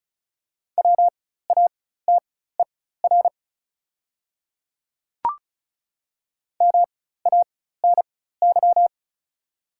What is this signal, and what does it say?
Morse code, heard as a keyed tone, about 36 wpm, spaced as for 16 wpm, 700 Hz. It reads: WATER MANY
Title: 4x Character Spacing - Top 100 Words - 35wpm